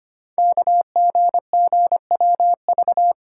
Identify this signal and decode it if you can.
Morse code, heard as a keyed tone, about 25 wpm, 700 Hz. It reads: KGGWV
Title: Code Groups - 5 Letters - 25wpm